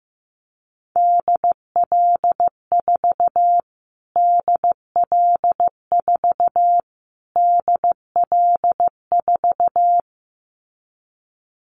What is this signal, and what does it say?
Morse code, heard as a keyed tone, about 15 wpm, 700 Hz. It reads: DL4 DL4 DL4
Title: Callsign Prefixes for Top 25 DXCC - Repeated 3x - 15wpm